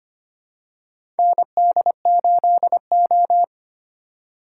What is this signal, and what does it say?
Morse code, heard as a keyed tone, about 25 wpm, 700 Hz. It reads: ND8O
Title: USA Call Signs - 4 Characters - 2x Character Spacing - 25wpm